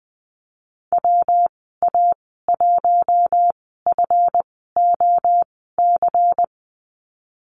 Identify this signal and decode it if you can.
Morse code, heard as a keyed tone, about 20 wpm, spaced as for 14 wpm, 700 Hz. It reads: WA1FOC
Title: 154 Lesson 25g - Letter p - Call Signs 20wpm